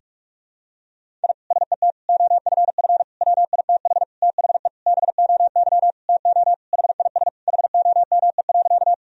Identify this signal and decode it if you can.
Morse code, heard as a keyed tone, about 46 wpm, 700 Hz. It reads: I SET OFF WITH THE BOY TO HIS HOME.
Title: Sentences from Top 1000 Words - Review Part 1 - 45wpm